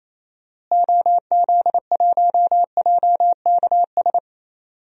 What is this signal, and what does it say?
Morse code, heard as a keyed tone, about 28 wpm, 700 Hz. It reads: OZ1JKS
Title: DXCC-100 Call Signs - 6 Characters - Repeated 2x - 28wpm